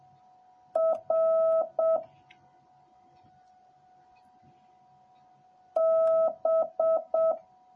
Morse code, heard as a keyed tone, about 7 wpm, 650 Hz. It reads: R B